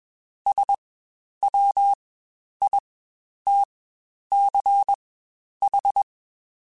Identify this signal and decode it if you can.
Morse code, heard as a keyed tone, about 21 wpm, spaced as for 10 wpm, 800 Hz. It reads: SWITCH
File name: Spaced / Chuckles Spaced-mp3s / MiddleAge sp @20